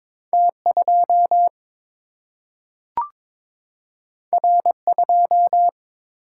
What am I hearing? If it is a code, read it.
Morse code, heard as a keyed tone, about 22 wpm, 700 Hz. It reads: T 2 R 2
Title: Single-Letter Single-Number 22wpm